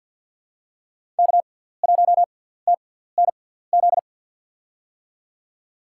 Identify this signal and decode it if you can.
Morse code, heard as a keyed tone, about 48 wpm, spaced as for 17 wpm, 700 Hz. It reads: K1TNZ
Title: USA Call Signs - 5 Characters - 6x Character Spacing - 50wpm